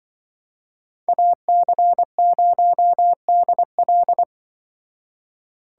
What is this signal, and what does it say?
Morse code, heard as a keyed tone, about 24 wpm, 700 Hz. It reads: AC0DL
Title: Speed Racing - USA Call Signs 20wpm